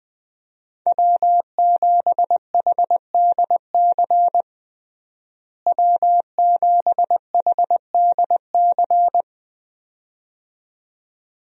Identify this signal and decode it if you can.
Morse code, heard as a keyed tone, about 20 wpm, 700 Hz. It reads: W7HDC W7HDC
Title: USA Call Signs - 5 Characters - Repeated 2x - 20wpm